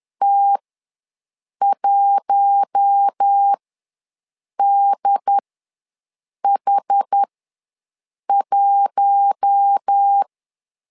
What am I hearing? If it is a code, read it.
Morse code, heard as a keyed tone, about 10 wpm, spaced as for 6 wpm, 800 Hz. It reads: T1DH1
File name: LEZIONI / 4 Q1 ws 10wpm 2538